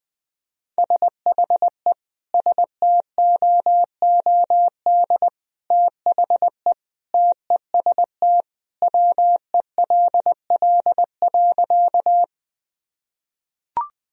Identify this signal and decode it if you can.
Morse code, heard as a keyed tone, about 20 wpm, 700 Hz. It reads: SHE STOOD THE TEST WELL.
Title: Sentences from Top 1000 Words - Review Part 3 - 20wpm